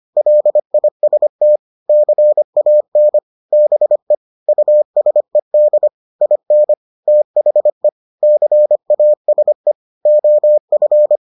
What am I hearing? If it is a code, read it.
Morse code, heard as a keyed tone, about 25 wpm, 600 Hz. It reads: LIST CAN BE USED IN THE CASE OF